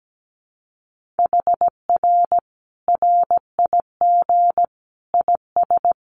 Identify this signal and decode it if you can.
Morse code, heard as a keyed tone, about 17 wpm, 700 Hz. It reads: HR RIG IS
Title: Common QSO Phrases 17wpm